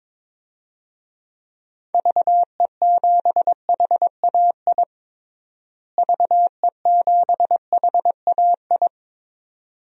Morse code, heard as a keyed tone, about 22 wpm, 700 Hz. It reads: VE7HAI VE7HAI
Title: DXCC-100 Call Signs - 6 Characters - Repeated 2x - 22wpm